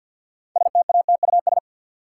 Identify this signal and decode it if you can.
Morse code, heard as a keyed tone, about 52 wpm, 700 Hz. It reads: STATUS
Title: Six Letter Words - 50wpm